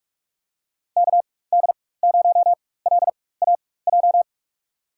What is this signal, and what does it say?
Morse code, heard as a keyed tone, about 44 wpm, spaced as for 21 wpm, 690 Hz. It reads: KD0LAJ
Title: USA Call Signs - 4x Character Spacing - 45wpm